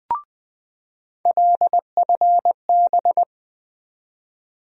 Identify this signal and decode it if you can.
Morse code, heard as a keyed tone, about 20 wpm, 700 Hz. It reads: LFB
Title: USA Call Sign Postfixes - 3 Letters 20wpm